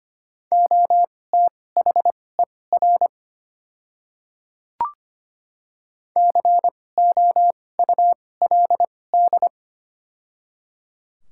Morse code, heard as a keyed tone, about 25 wpm, spaced as for 18 wpm, 700 Hz. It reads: OTHER COULD
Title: 2x Character Spacing - Top 100 Words - 25wpm